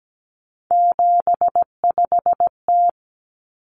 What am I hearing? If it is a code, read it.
Morse code, heard as a keyed tone, about 17 wpm, 700 Hz. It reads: 75T
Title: Common Cut Numbers - 3-Digit 17wpm